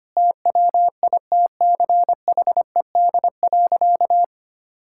Morse code, heard as a keyed tone, about 25 wpm, 700 Hz. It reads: TWITCHED.
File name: practice / 25wpm / Sentences with 2-Word Phrases - 25wpm